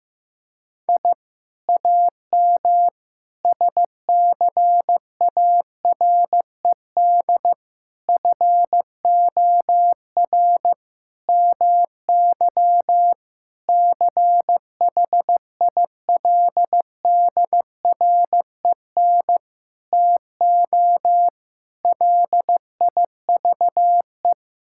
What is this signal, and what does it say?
Morse code, heard as a keyed tone, about 16 wpm, 700 Hz. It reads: I AM SCARED FOR MY CHILDREN TO LIVE
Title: Sentences from Top 1000 Words - Review Part 5 - 15wpm